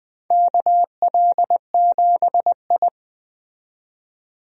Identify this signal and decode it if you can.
Morse code, heard as a keyed tone, about 21 wpm, 700 Hz. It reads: KL7I
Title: DXCC-100 Call Signs - 4 Characters - 20wpm